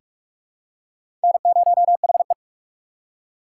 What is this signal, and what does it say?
Morse code, heard as a keyed tone, about 46 wpm, 700 Hz. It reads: N0HE
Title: USA Call Signs - 4 Characters - 4x Character Spacing - 45wpm